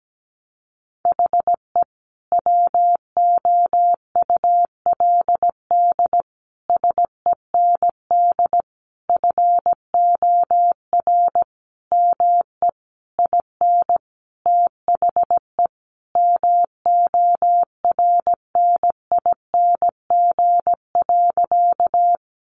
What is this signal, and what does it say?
Morse code, heard as a keyed tone, about 17 wpm, 700 Hz. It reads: HE WOULD SEND FOR ME IN THE MORNING.
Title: Sentences from Top 1500 Words 17wpm